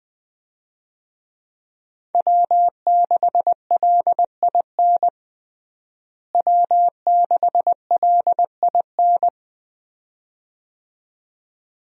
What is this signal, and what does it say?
Morse code, heard as a keyed tone, about 20 wpm, 700 Hz. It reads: W6LIN W6LIN
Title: USA Call Signs - 5 Characters - Repeated 2x - 20wpm